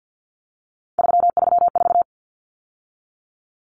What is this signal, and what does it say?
Morse code, heard as a keyed tone, about 50 wpm, 700 Hz. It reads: 334
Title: Numbers - 3-Digit 50wpm